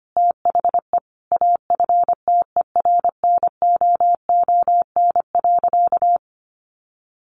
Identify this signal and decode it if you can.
Morse code, heard as a keyed tone, about 25 wpm, 700 Hz. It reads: THE AFTERNOON.
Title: Sentences from Top 1500 Words 25wpm